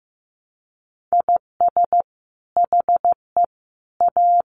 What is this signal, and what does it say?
Morse code, heard as a keyed tone, about 15 wpm, 700 Hz. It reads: IS HE A
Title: Sentences from Top 300 Words 15wpm